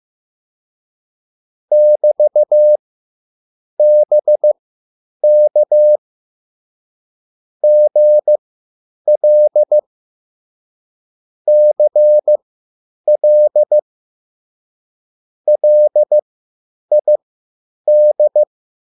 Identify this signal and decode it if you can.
Morse code, heard as a keyed tone, about 15 wpm, spaced as for 8 wpm, 600 Hz. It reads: <BT>BK GL CL LID
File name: QSL / CW lesson